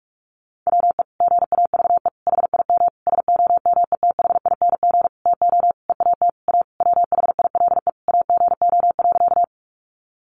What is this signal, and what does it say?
Morse code, heard as a keyed tone, about 44 wpm, 700 Hz. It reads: WE GAVE HIM SOMETHING TO EAT A WHILE AGO.
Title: Sentences from Top 600 Words 45wpm